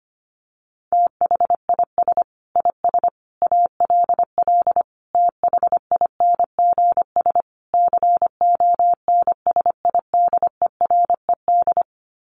Morse code, heard as a keyed tone, about 25 wpm, 700 Hz. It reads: THIS IS ALL THINGS CONSIDERED
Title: Sets of 5 Words 25wpm